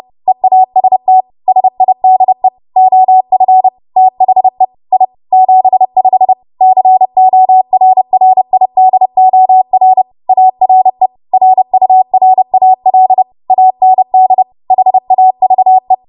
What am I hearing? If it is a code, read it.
Morse code, heard as a keyed tone, about 30 wpm, 750 Hz. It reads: EAST SIDE OF THE I 75 CORRIDOR ARE RURAL AND HAVE
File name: Morse / 30WPM